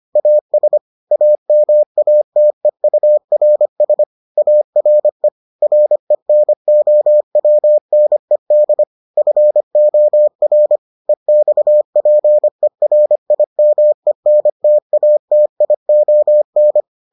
Morse code, heard as a keyed tone, about 25 wpm, 600 Hz. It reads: AS AMATEURS ARE RENOWNED FOR EXPERIMENTATION